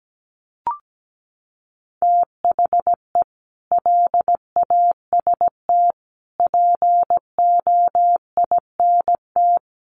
Morse code, heard as a keyed tone, about 17 wpm, 700 Hz. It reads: THE LAST POINT
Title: Sentences from Top 1200 Words 17wpm